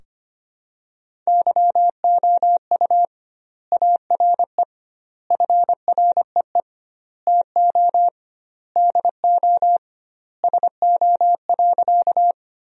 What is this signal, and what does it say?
Morse code, heard as a keyed tone, about 25 wpm, 700 Hz. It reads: YOU ARE FREE TO DO SO.
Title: 2x Word-Spacing - Sentences from Top 500 Words - 25wpm